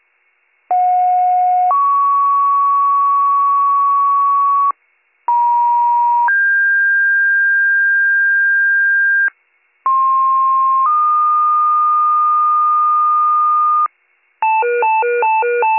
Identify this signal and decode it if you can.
Two-tone page, heard as a one-second tone, then a three-second tone, about 730 then 1120 Hz, about 950 then 1600 Hz, about 1060 then 1220 Hz. It is heard from 0.7 to 13.9 s.